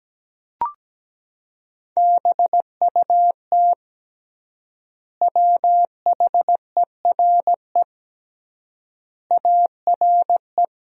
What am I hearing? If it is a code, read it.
Morse code, heard as a keyed tone, about 17 wpm, 700 Hz. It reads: BUT WHERE ARE
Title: 3x Word-Spacing - Sentences from Top 200 Words - 17wpm